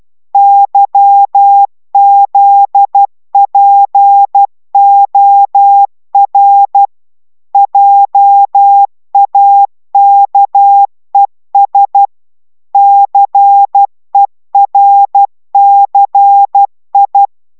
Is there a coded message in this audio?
Yes: Morse code from 0.3 to 17.3 s, heard as a keyed tone, about 12 wpm, 800 Hz. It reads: YZPOR JAKES CERCI